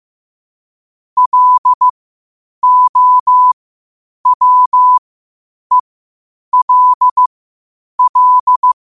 Morse code, heard as a keyed tone, about 15 wpm, spaced as for 8 wpm, 1000 Hz. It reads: LOWELL
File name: qotdfiles / 8wpm